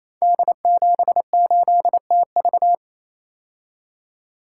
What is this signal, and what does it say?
Morse code, heard as a keyed tone, about 27 wpm, 700 Hz. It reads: D78TV
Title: Code Groups - 5 Letter:Numbers - 28wpm